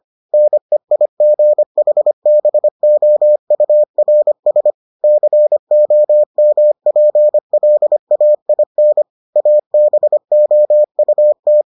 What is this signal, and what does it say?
Morse code, heard as a keyed tone, about 25 wpm, 600 Hz. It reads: NEIGHBOURS COMPLAIN ABOUT